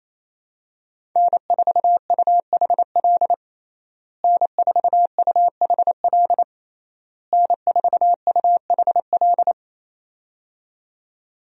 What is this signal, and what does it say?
Morse code, heard as a keyed tone, about 28 wpm, 700 Hz. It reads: N4UHL N4UHL N4UHL
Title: USA Call Signs - 5 Characters - Repeated 3x - 28wpm